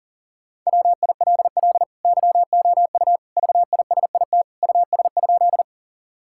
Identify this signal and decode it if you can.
Morse code, heard as a keyed tone, about 40 wpm, 700 Hz. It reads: WILL YOU VISIT US?
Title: Sentences from Top 1000 Words - Review Part 6 - 40wpm